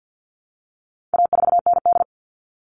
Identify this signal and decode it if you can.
Morse code, heard as a keyed tone, about 48 wpm, 700 Hz. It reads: A4ND
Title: All Cut Numbers - 4-Digit 50wpm